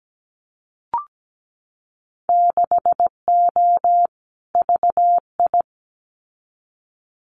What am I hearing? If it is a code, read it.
Morse code, heard as a keyed tone, about 17 wpm, 700 Hz. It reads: 6O VI
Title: Winter Field Day Exchanges Encore 17wpm